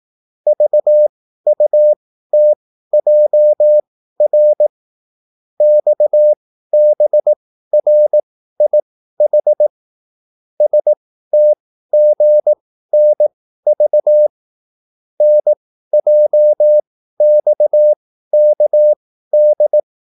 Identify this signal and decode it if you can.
Morse code, heard as a keyed tone, about 18 wpm, spaced as for 13 wpm, 600 Hz. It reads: VUTJR XBRIH STGNV NJXKD